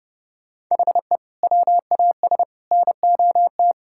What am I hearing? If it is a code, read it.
Morse code, heard as a keyed tone, about 30 wpm, 700 Hz. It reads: HE WAS NOT
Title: Sentences from Top 1600 Words 30wpm